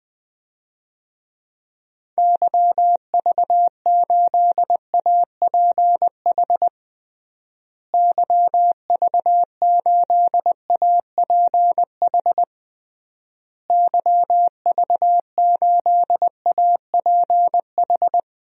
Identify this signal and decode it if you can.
Morse code, heard as a keyed tone, about 20 wpm, 700 Hz. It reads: YV8APH YV8APH YV8APH
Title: DXCC-100 Call Signs - 6 Characters - Repeated 3x - 20wpm